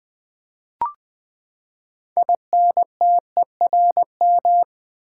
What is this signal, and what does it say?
Morse code, heard as a keyed tone, about 20 wpm, 700 Hz. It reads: INTERM
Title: Build A Word - 6 Letters At A Time - 12-Letter Words - 20wpm